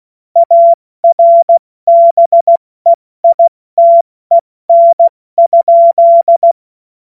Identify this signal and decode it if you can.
Morse code, heard as a keyed tone, about 16 wpm, 680 Hz. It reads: ARBEITEN?
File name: Morsekurs / Witze / Witz 24.Oktober 2023 70 F80